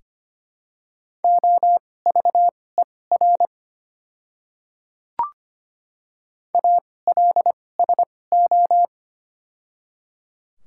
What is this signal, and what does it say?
Morse code, heard as a keyed tone, about 25 wpm, spaced as for 18 wpm, 690 Hz. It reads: OVER ALSO
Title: 2x Character Spacing - Top 200 Words - 25wpm